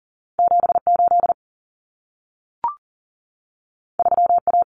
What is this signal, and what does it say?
Morse code, heard as a keyed tone, about 40 wpm, 700 Hz. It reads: 78 3A
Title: Common Cut Numbers - 2-Digit 40wpm